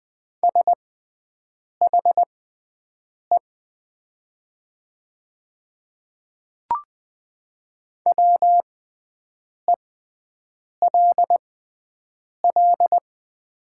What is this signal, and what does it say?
Morse code, heard as a keyed tone, about 20 wpm, spaced as for 7 wpm, 700 Hz. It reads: SHE WELL
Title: 6x Character Spacing - Top 300 Words - 20wpm